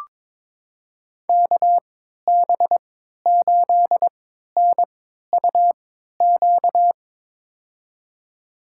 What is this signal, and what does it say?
Morse code, heard as a keyed tone, about 21 wpm, spaced as for 12 wpm, 700 Hz. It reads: KB8NUQ
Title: USA Call Signs - 6 Characters - 3x Character Spacing - 22wpm